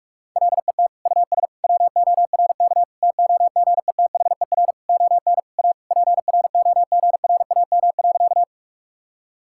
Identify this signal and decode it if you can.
Morse code, heard as a keyed tone, about 46 wpm, 700 Hz. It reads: LET US WORK TOGETHER ON A PROGRAM.
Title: Sentences from Top 1000 Words - Review Part 5 - 45wpm